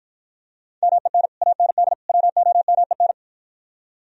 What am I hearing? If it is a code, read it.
Morse code, heard as a keyed tone, about 52 wpm, 700 Hz. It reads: MEN AND WOMEN